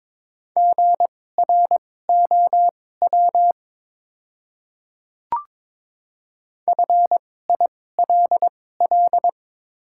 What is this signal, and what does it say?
Morse code, heard as a keyed tone, about 23 wpm, spaced as for 16 wpm, 700 Hz. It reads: GROW FILL
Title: 2x Character Spacing - Top 500 Words - 22wpm